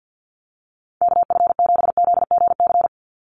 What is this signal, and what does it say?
Morse code, heard as a keyed tone, about 50 wpm, 700 Hz. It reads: KF7ZGC
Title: USA Call Signs 50wpm